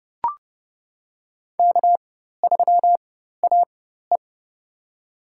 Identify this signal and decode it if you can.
Morse code, heard as a keyed tone, about 31 wpm, spaced as for 14 wpm, 700 Hz. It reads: K3AE
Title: USA Call Signs - 4 Characters - 4x Character Spacing - 30wpm